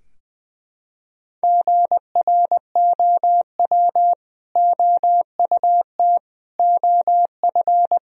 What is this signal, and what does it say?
Morse code, heard as a keyed tone, about 20 wpm, 700 Hz. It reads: GROW OUT OF
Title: Sentences from Top 1000 Words - Review Part 1 - 20wpm